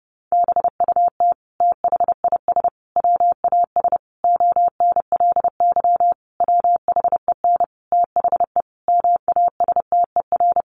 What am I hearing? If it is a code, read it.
Morse code, heard as a keyed tone, about 28 wpm, 700 Hz. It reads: BUT THIS WAS ONLY WHEN THE MASTER
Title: Sentences from Top 1300 Words 30wpm